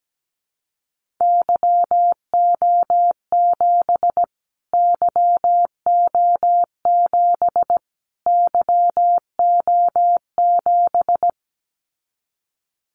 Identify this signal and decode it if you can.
Morse code, heard as a keyed tone, about 17 wpm, 700 Hz. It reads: YO7 YO7 YO7
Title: Callsign Prefixes for Top 25 DXCC - Repeated 3x - 17wpm